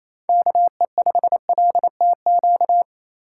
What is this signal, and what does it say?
Morse code, heard as a keyed tone, about 33 wpm, 700 Hz. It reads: KE5LTQ
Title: Speed Racing - USA 2x3 Call Signs 28wpm